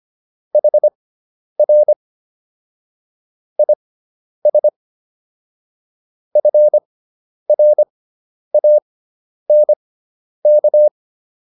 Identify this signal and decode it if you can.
Morse code, heard as a keyed tone, about 25 wpm, spaced as for 10 wpm, 600 Hz. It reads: HR IS FRANK
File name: QSO208 10